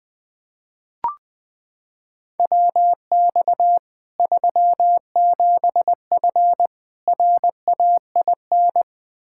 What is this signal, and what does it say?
Morse code, heard as a keyed tone, about 20 wpm, 700 Hz. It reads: WX 37F RAIN
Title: Practice QSO - Exchange 2 - Info - 20wpm